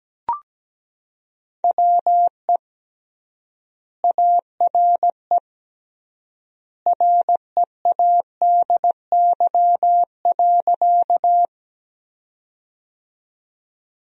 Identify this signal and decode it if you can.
Morse code, heard as a keyed tone, about 17 wpm, 700 Hz. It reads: WE ARE READY.
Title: 3x Word-Spacing - Sentences from Top 400 Words - 17wpm